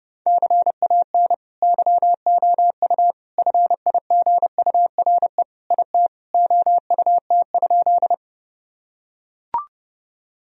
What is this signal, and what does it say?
Morse code, heard as a keyed tone, about 30 wpm, 700 Hz. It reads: CAN YOU FIGURE IT OUT?
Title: Sentences from Top 1000 Words - Review Part 2 - 30wpm